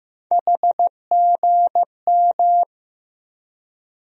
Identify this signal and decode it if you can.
Morse code, heard as a keyed tone, about 15 wpm, 700 Hz. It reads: HGM
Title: USA Call Sign Postfixes - 3 Letters 15wpm